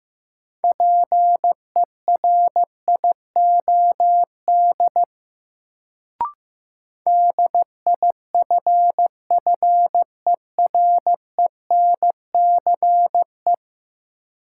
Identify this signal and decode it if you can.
Morse code, heard as a keyed tone, about 15 wpm, 700 Hz. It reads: PERIOD DIFFERENCE
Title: Sentences from Top 800 Words - New Word Drill - 15wpm